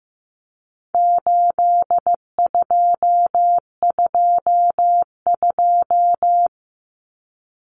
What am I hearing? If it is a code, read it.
Morse code, heard as a keyed tone, about 15 wpm, 700 Hz. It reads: 8222
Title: Common Cut Numbers - 4-Digit 15wpm